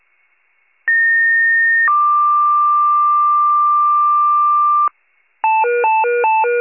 Two-tone page, heard as a one-second tone, then a three-second tone, about 1850 then 1180 Hz. It is heard from 0.9 to 4.9 s.
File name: wcalert